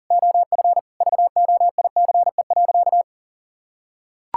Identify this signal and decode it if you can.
Morse code, heard as a keyed tone, about 40 wpm, 700 Hz. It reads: OF VOICE.
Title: Sentences from Top 1600 Words 40wpm